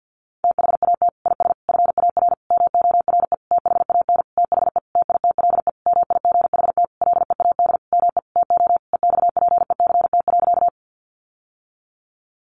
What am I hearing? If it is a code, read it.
Morse code, heard as a keyed tone, about 48 wpm, 700 Hz. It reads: THAT IS FAR MORE THAN THE TITLE MIGHT LEAD ME TO EXPECT.
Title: Sentences from Top 1200 Words 50wpm